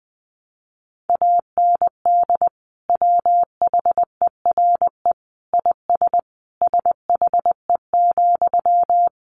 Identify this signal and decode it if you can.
Morse code, heard as a keyed tone, about 20 wpm, 700 Hz. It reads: AND WHERE IS SHE,
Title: Sentences from Top 200 Words 20wpm